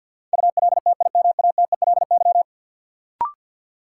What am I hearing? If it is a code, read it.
Morse code, heard as a keyed tone, about 52 wpm, 700 Hz. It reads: ULTIMATELY
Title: Sentences from Top 1200 Words - New Word Drill - 50wpm